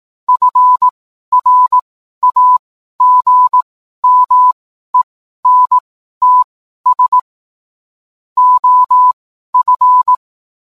Morse code, heard as a keyed tone, about 18 wpm, spaced as for 12 wpm, 1000 Hz. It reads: FRAGMENTS OF